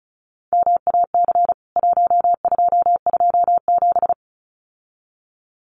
Morse code, heard as a keyed tone, about 34 wpm, 700 Hz. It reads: MAC 1227
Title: CWT Exchange 35wpm